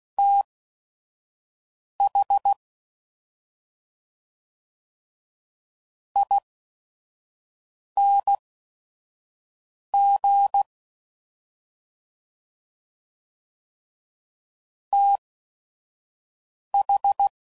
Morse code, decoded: THING TH

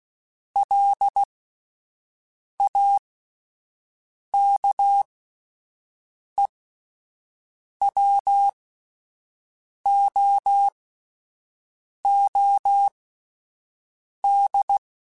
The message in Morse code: LAKEWOOD

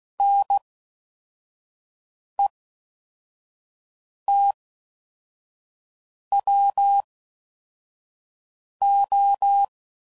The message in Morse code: NETWO

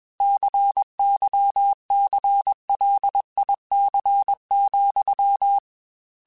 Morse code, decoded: CYCLIC,

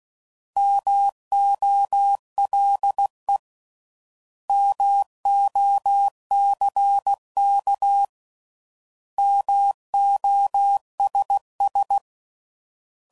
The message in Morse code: MOLE MOCK MOSS

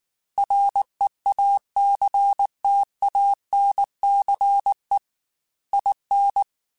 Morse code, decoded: REACTANCE IN

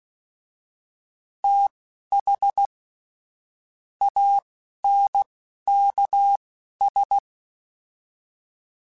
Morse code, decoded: TH ANKS

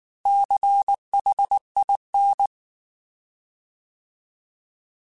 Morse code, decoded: CHIN

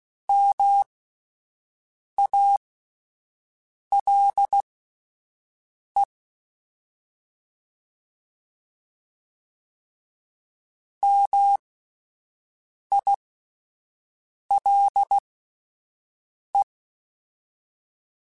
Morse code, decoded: MALE MILE

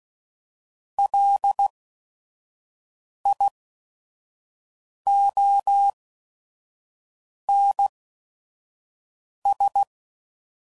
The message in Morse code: LIONS

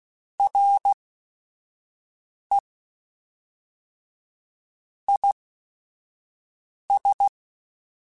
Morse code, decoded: R E I S

